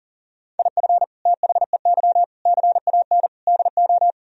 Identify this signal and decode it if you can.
Morse code, heard as a keyed tone, about 40 wpm, 700 Hz. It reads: IF THEY CAN DO
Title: Sentences from Top 1000 Words - Review Part 4 - 40wpm